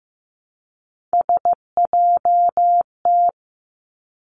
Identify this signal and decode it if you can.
Morse code, heard as a keyed tone, about 15 wpm, 700 Hz. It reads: SJT
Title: Three Challenging Letter-Number Combinations 15wpm